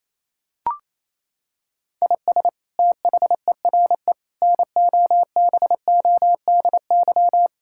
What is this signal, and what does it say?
Morse code, heard as a keyed tone, about 28 wpm, 700 Hz. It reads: IS THERE NOBODY
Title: Sentences from Top 1600 Words 28wpm